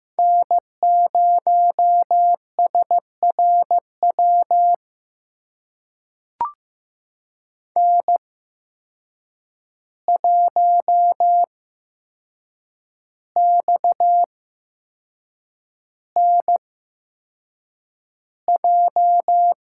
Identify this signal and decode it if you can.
Morse code, heard as a keyed tone, about 15 wpm, spaced as for 6 wpm, 700 Hz. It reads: N0SRW N1XNJ
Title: USA Call Signs - 5 Characters - 8x Character Spacing - 15wpm